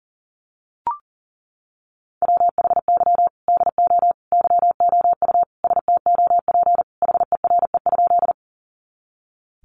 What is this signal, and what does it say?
Morse code, heard as a keyed tone, about 40 wpm, 700 Hz. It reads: WHY DO YOU STOP HERE?
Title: Sentences from Top 300 Words 40wpm